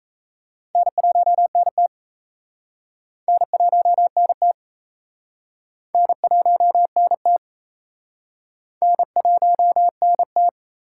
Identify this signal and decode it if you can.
Morse code, heard as a keyed tone, about 35 wpm, 700 Hz. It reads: N1NT N1NT N1NT N1NT